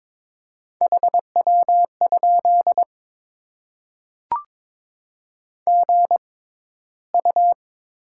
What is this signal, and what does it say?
Morse code, heard as a keyed tone, about 22 wpm, spaced as for 14 wpm, 690 Hz. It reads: HW? GU
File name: Top 20 QSO Elements - 6x Character Spacing - 22wpm